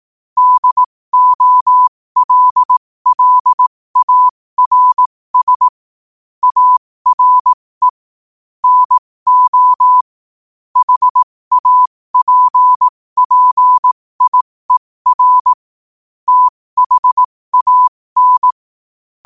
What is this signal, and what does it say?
Morse code, heard as a keyed tone, about 18 wpm, spaced as for 15 wpm, 1000 Hz. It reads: DOLLARS ARE NO HAPPIER THAN
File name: qotdfiles / 15wpm